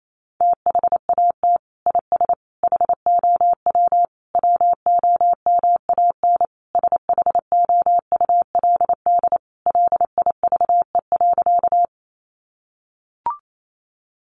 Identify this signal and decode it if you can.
Morse code, heard as a keyed tone, about 28 wpm, 700 Hz. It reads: THAT IS HOW WOMAN SHOULD LIVE.
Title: Sentences from Top 1100 Words 28wpm